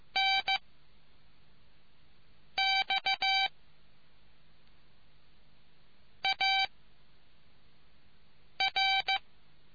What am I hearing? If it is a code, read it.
Morse code, heard as a keyed tone, about 15 wpm, 770 Hz. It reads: N X A R